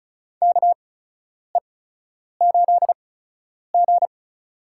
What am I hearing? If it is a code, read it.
Morse code, heard as a keyed tone, about 36 wpm, spaced as for 10 wpm, 700 Hz. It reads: KE8G
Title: USA Call Signs - 8x Character Spacing - 35wpm